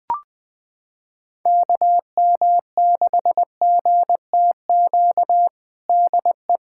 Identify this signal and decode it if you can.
Morse code, heard as a keyed tone, about 21 wpm, 700 Hz. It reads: KM6GTQ DE
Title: USA Callsign DE Callsign - 6 Characters - 20wpm